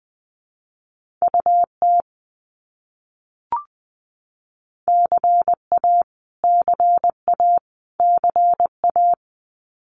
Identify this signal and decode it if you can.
Morse code, heard as a keyed tone, about 20 wpm, 700 Hz. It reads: UT CA CA CA
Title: US States Repeated 3x 20wpm